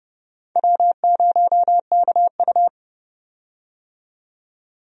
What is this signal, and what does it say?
Morse code, heard as a keyed tone, about 32 wpm, 700 Hz. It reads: W0KU
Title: USA Call Signs - 4 Characters - 30wpm